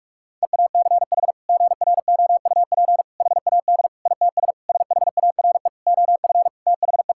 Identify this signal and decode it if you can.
Morse code, heard as a keyed tone, about 46 wpm, 690 Hz. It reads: EACH GROUP HAD ITS SHARE OF THE
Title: Sentences from Top 1000 Words - Review Part 6 - 45wpm